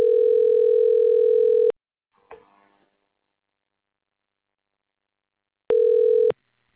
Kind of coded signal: telephone call progress tone